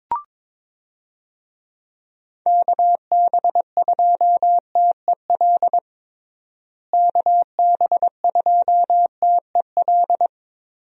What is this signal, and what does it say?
Morse code, heard as a keyed tone, about 21 wpm, 700 Hz. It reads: KB2TEL KB2TEL
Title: USA Call Signs - 6 Characters - Repeated 3x - 22wpm